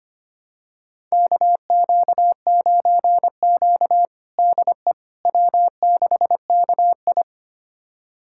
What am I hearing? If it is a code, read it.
Morse code, heard as a keyed tone, about 25 wpm, 700 Hz. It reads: KQ9Q DE W6KI
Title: USA Callsign DE Callsign - 4 Characters - 25wpm